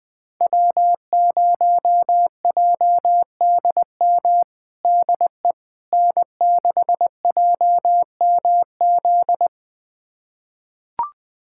Morse code, heard as a keyed tone, about 20 wpm, 700 Hz. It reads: W0JDM DE N6JMZ
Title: USA Callsign DE Callsign 20wpm